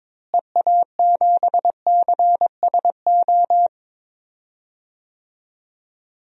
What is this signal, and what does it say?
Morse code, heard as a keyed tone, about 22 wpm, 700 Hz. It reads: EA7CSO
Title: DXCC-100 Call Signs - 6 Characters - Repeated 3x - 22wpm